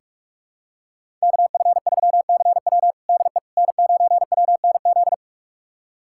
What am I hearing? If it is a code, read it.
Morse code, heard as a keyed tone, about 44 wpm, 700 Hz. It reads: KF3CW DE N9WNZ
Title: USA Callsign DE Callsign - 5 Characters - 45wpm